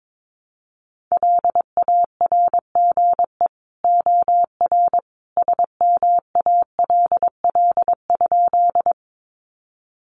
Morse code, heard as a keyed tone, about 22 wpm, 700 Hz. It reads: LARGE OR SMALL?
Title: Sentences from Top 200 Words 22wpm